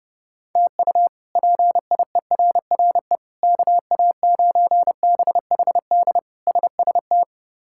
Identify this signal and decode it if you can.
Morse code, heard as a keyed tone, about 30 wpm, 700 Hz. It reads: TU PIERRE KA9BHD SST
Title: K1USN SST QSO Practice - 30wpm